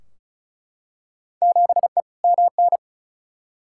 Morse code, heard as a keyed tone, about 34 wpm, 700 Hz. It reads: 7E MN
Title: Field Day Exchanges Encore - 35wpm